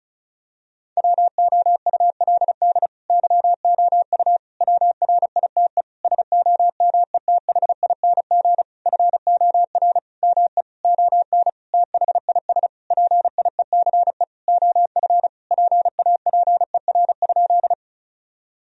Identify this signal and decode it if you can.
Morse code, heard as a keyed tone, about 36 wpm, 700 Hz. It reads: WOULD YOU WRITE SOMETHING FOR ME ON THIS PIECE OF PAPER?
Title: Sentences from Top 1000 Words - Review Part 2 - 35wpm